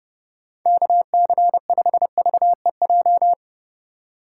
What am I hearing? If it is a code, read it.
Morse code, heard as a keyed tone, about 32 wpm, 700 Hz. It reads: KC5VEJ
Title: USA Call Signs - 6 Characters - 3x Character Spacing - 30wpm